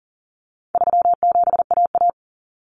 Morse code, heard as a keyed tone, about 37 wpm, 700 Hz. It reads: 37AA
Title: Common Cut Numbers - 4-Digit 40wpm